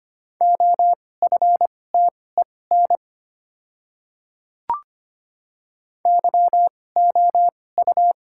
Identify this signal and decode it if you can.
Morse code, heard as a keyed tone, about 25 wpm, spaced as for 18 wpm, 690 Hz. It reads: OFTEN YOU
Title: 2x Character Spacing - Top 300 Words - 25wpm